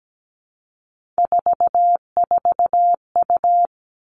Morse code, heard as a keyed tone, about 17 wpm, 700 Hz. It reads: 44U
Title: All Cut Numbers - 3-Digit 17wpm